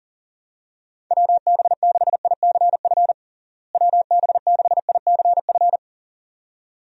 Morse code, heard as a keyed tone, about 40 wpm, 700 Hz. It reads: WB6ICF WB6ICF
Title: USA Call Signs - 6 Characters - Repeated 2x - 40wpm